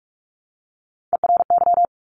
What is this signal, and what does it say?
Morse code, heard as a keyed tone, about 46 wpm, 690 Hz. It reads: ERY
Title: Top Three Letter Combinations in Words 45wpm